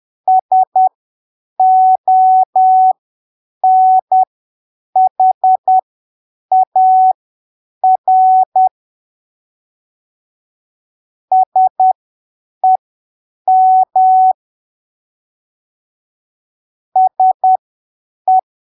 Morse code, heard as a keyed tone, about 10 wpm, spaced as for 7 wpm, 750 Hz. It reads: SONHAR SEM SE
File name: arquivos / texto / texto 21 10ppm